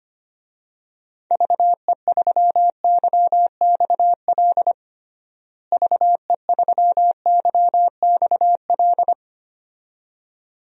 Morse code, heard as a keyed tone, about 25 wpm, 700 Hz. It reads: VE3YXL VE3YXL
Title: DXCC-100 Call Signs - 6 Characters - Repeated 2x - 25wpm